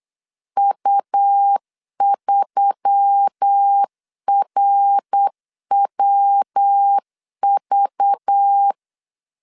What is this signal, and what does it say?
Morse code, heard as a keyed tone, about 8 wpm, 800 Hz. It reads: U3RWV
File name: LEZIONI / 6 PKX0 / 6 PKX 8wpm 3041